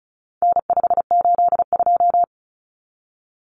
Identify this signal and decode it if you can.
Morse code, heard as a keyed tone, about 34 wpm, 700 Hz. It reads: N582
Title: Common Cut Numbers - 4-Digit 35wpm